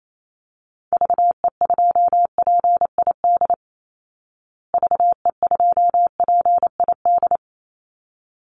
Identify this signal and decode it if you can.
Morse code, heard as a keyed tone, about 28 wpm, 700 Hz. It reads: VE2PID VE2PID
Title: Call Signs Repeated 3x - 28wpm